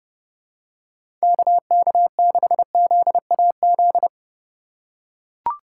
Morse code, heard as a keyed tone, about 30 wpm, 700 Hz. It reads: KK6ZAZ
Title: USA Call Signs - 4x Character Spacing - 30wpm